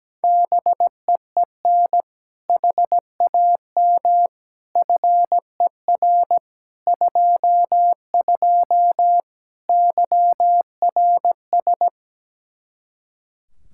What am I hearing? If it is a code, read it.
Morse code, heard as a keyed tone, about 17 wpm, 700 Hz. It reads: BEEN HAM FER 22 YRS
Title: Practice QSO - Exchange 3 - More Info - 17wpm